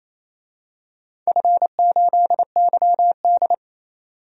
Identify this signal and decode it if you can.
Morse code, heard as a keyed tone, about 28 wpm, 700 Hz. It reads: F8YD
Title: DXCC-100 Call Signs - 4 Characters - 28wpm